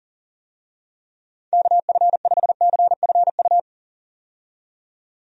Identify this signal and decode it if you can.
Morse code, heard as a keyed tone, about 40 wpm, 700 Hz. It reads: KF5CFU
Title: USA Call Signs - 6 Characters - 40wpm